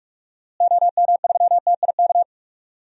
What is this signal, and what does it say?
Morse code, heard as a keyed tone, about 46 wpm, 700 Hz. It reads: OM3TIK
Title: DXCC-100 Call Signs - 6 Characters - Repeated 3x - 45wpm